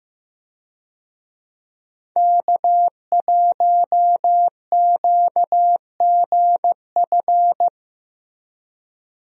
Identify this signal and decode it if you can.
Morse code, heard as a keyed tone, about 15 wpm, 700 Hz. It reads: K1QGF
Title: USA Call Signs - 5 Characters - Repeated 3x - 15wpm